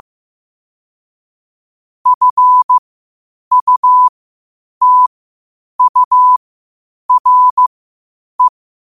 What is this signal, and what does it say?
Morse code, heard as a keyed tone, about 15 wpm, spaced as for 8 wpm, 1000 Hz. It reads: FUTURE
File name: qotdfiles / 8wpm